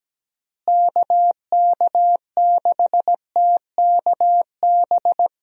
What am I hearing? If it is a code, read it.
Morse code, heard as a keyed tone, about 17 wpm, 700 Hz. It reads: KK6TKB
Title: USA Call Signs - 6 Characters - Repeated 3x - 17wpm